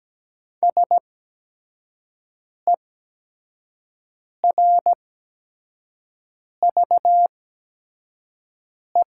Morse code, heard as a keyed tone, about 17 wpm, spaced as for 5 wpm, 700 Hz. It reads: SERVE
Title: 8x Character Spacing - Top 500 Words - 17wpm